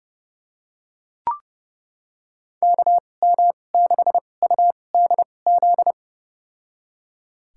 Morse code, heard as a keyed tone, about 30 wpm, spaced as for 22 wpm, 700 Hz. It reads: KM6UDZ
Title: USA Call Signs - 6 Characters - 2x Character Spacing - 30wpm